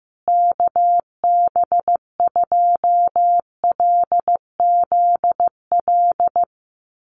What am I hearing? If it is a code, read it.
Morse code, heard as a keyed tone, about 15 wpm, 700 Hz. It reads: KB2LZL
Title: USA Call Signs Repeated 3x 15wpm